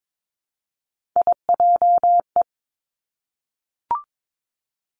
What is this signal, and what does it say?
Morse code, heard as a keyed tone, about 22 wpm, 700 Hz. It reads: IJE
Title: Three Challenging Letter-Number Combinations 22wpm